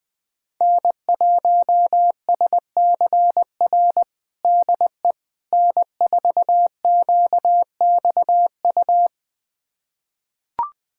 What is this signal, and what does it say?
Morse code, heard as a keyed tone, about 20 wpm, 700 Hz. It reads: N1SCR DE N4QXU
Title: USA Callsign DE Callsign - 5 Characters - 20wpm